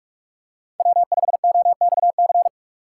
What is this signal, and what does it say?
Morse code, heard as a keyed tone, about 42 wpm, 700 Hz. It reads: W5OXC